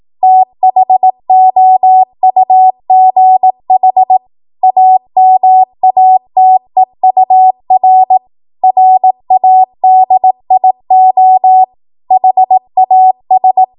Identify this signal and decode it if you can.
Morse code, heard as a keyed tone, about 18 wpm, 750 Hz. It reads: THOUGH AMATEUR RADIO HAS
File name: Morse / 18WPM